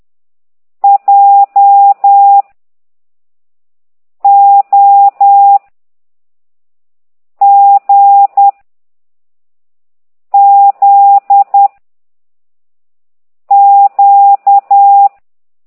Morse code, heard as a keyed tone, about 10 wpm, 800 Hz. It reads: J O G Z Q
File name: ct2idw / exerc15